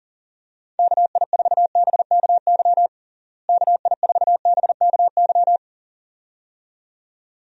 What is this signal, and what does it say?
Morse code, heard as a keyed tone, about 40 wpm, 700 Hz. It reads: KI4BKY KI4BKY
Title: USA Call Signs - 6 Characters - Repeated 2x - 40wpm